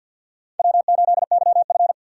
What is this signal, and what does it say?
Morse code, heard as a keyed tone, about 52 wpm, 700 Hz. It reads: W8YF
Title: USA Call Signs - 4 Characters - 2x Character Spacing - 50wpm